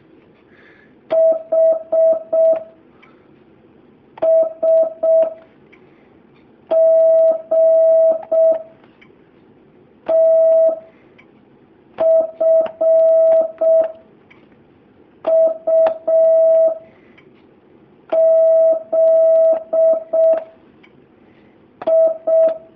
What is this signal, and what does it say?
Morse code, heard as a keyed tone, about 6 wpm, 650 Hz. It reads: H S G T F U Z I
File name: jh8cbh / 2019cw